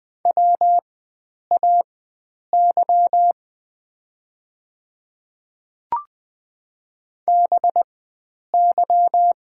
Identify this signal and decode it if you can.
Morse code, heard as a keyed tone, about 20 wpm, spaced as for 9 wpm, 700 Hz. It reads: WAY BY
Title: 4x Character Spacing - Top 100 Words - 20wpm